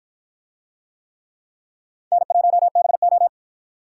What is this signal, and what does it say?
Morse code, heard as a keyed tone, about 51 wpm, 700 Hz. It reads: N1BO